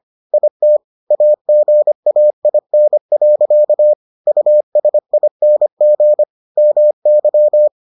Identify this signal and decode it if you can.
Morse code, heard as a keyed tone, about 25 wpm, 600 Hz. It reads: IT AGAIN. USING MY